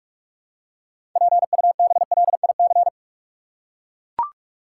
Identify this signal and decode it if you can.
Morse code, heard as a keyed tone, about 43 wpm, 700 Hz. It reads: PUBLIC
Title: Six Letter Words - 45wpm